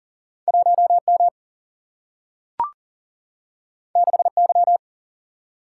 Morse code, heard as a keyed tone, about 40 wpm, 700 Hz. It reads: 1M 6Y
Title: Single-Letter Single-Number 40wpm